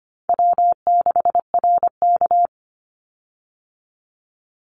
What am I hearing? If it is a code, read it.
Morse code, heard as a keyed tone, about 25 wpm, 700 Hz. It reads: W6RK
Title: Call Signs Repeated 3x - 25wpm